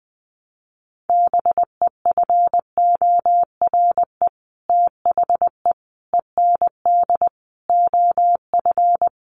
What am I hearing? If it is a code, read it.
Morse code, heard as a keyed tone, about 20 wpm, 700 Hz. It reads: BEFORE THE END OF
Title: Sets of 5 Words Encore 20wpm